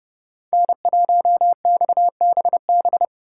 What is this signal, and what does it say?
Morse code, heard as a keyed tone, about 30 wpm, 700 Hz. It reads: N1XBB
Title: USA Call Signs - 4x Character Spacing - 30wpm